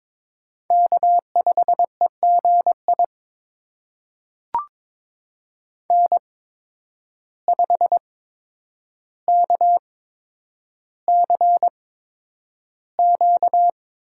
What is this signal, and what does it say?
Morse code, heard as a keyed tone, about 22 wpm, spaced as for 9 wpm, 700 Hz. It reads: K5EGI N5KCQ
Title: USA Call Signs - 5 Characters - 8x Character Spacing - 22wpm